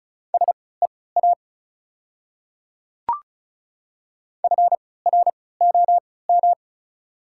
Morse code, heard as a keyed tone, about 36 wpm, spaced as for 20 wpm, 700 Hz. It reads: SEA FROM
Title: 3x Character Spacing - Top 500 Words - 35wpm